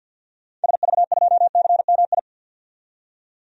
Morse code, heard as a keyed tone, about 49 wpm, 700 Hz. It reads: SV2CMI
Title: DXCC-100 Call Signs - 6 Characters - 50wpm